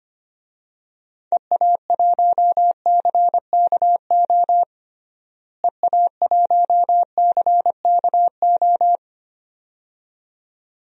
Morse code, heard as a keyed tone, about 25 wpm, 700 Hz. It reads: EA1CKO EA1CKO
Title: DXCC-100 Call Signs - 6 Characters - Repeated 2x - 25wpm